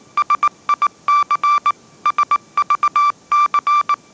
Morse code, heard as a keyed tone, about 19 wpm, 1170 Hz. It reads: SIC SVC